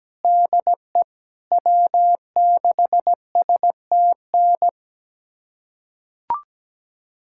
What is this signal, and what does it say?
Morse code, heard as a keyed tone, about 17 wpm, 700 Hz. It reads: DE W6STN
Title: USA Callsign DE Callsign 17wpm